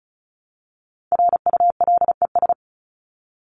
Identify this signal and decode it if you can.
Morse code, heard as a keyed tone, about 34 wpm, 700 Hz. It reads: RULES
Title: ES:S Suffix - Top 100 Words with Base Words 35wpm